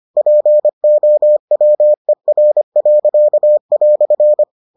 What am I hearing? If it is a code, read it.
Morse code, heard as a keyed tone, about 25 wpm, 600 Hz. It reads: POWER."